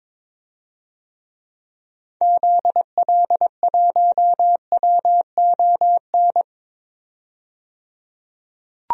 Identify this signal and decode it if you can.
Morse code, heard as a keyed tone, about 21 wpm, 700 Hz. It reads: ZL1WON
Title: DXCC-100 Call Signs - 6 Characters - 22wpm